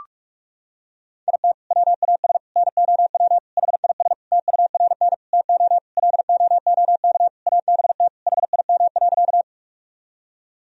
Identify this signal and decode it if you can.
Morse code, heard as a keyed tone, about 41 wpm, 700 Hz. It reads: IT WAS NOW HIS TURN TO LOOK ABT HIM.
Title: Sentences from Top 200 Words and Abbreviations 45wpm